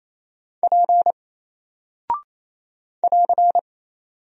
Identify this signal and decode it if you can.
Morse code, heard as a keyed tone, about 28 wpm, 700 Hz. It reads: P <AR>
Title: Single Character (Letter:Number:Punctuation:Prosign) Alt 28wpm